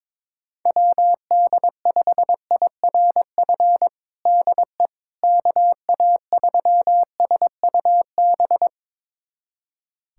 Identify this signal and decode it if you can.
Morse code, heard as a keyed tone, about 22 wpm, 700 Hz. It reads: WD5IRF DE KA3SUB
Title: USA Callsign DE Callsign - 6 Characters - 22wpm